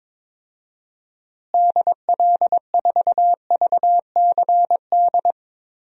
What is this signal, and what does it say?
Morse code, heard as a keyed tone, about 22 wpm, 700 Hz. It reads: DL4VCD
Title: DXCC-100 Call Signs - 6 Characters - Repeated 2x - 22wpm